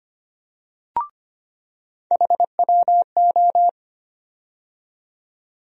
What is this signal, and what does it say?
Morse code, heard as a keyed tone, about 25 wpm, 700 Hz. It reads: HWO
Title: USA Call Sign Postfixes - 3 Letters 25wpm